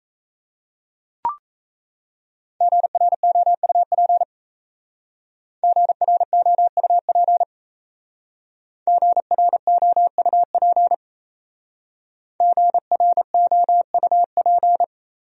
Morse code, heard as a keyed tone, about 36 wpm, 700 Hz. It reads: GROUP GROUP GROUP GROUP